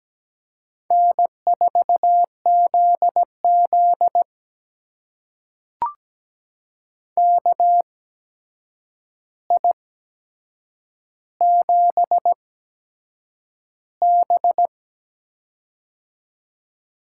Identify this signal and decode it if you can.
Morse code, heard as a keyed tone, about 17 wpm, spaced as for 7 wpm, 700 Hz. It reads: N4ZZ KI7B
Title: USA Call Signs - 4 Characters - 8x Character Spacing - 17wpm